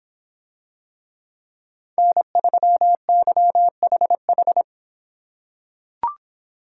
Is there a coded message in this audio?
Yes: Morse code from 2.0 to 4.6 s, heard as a keyed tone, about 26 wpm, 700 Hz. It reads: N3YHH